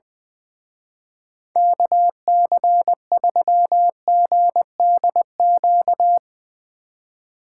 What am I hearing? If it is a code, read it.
Morse code, heard as a keyed tone, about 22 wpm, 700 Hz. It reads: KC3GDQ
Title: Speed Racing - USA 2x3 Call Signs 20wpm